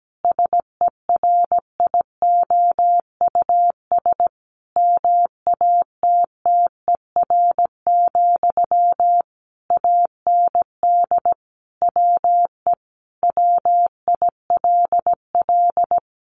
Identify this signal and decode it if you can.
Morse code, heard as a keyed tone, about 17 wpm, 700 Hz. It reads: SERIOUS MATTER, AND WE WILL
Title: Sentences from Top 1100 Words 17wpm